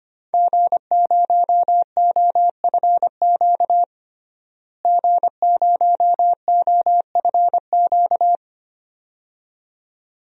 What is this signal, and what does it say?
Morse code, heard as a keyed tone, about 25 wpm, 700 Hz. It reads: G0OFQ G0OFQ
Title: DXCC-100 Call Signs - 5 Characters - Repeated 2x - 25wpm